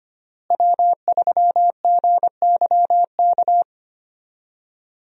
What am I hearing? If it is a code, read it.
Morse code, heard as a keyed tone, about 25 wpm, 700 Hz. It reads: W3GYK
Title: USA Call Signs - 5 Characters - Repeated 3x - 25wpm